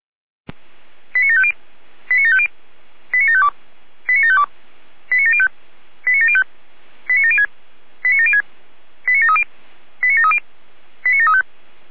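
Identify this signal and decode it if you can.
Five-tone selective call (ZVEI), heard as quick runs of five tones, tones of about 70 ms. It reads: 89750 89750 89752 89752 89705 89705 89706 89706 89730 89730 89735
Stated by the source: ZVEI five-tone selective call received with an SDR.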